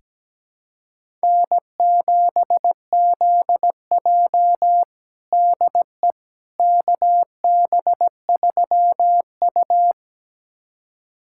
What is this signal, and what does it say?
Morse code, heard as a keyed tone, about 17 wpm, 700 Hz. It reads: N7ZJ DE KB3U
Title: USA Callsign DE Callsign - 4 Characters - 17wpm